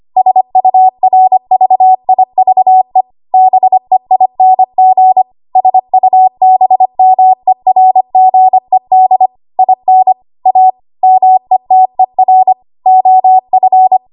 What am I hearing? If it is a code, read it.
Morse code, heard as a keyed tone, about 25 wpm, 750 Hz. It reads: SURVIVE BEING SUBMERGED IN A METER OF
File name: Archive / 25 WPM / 25WPM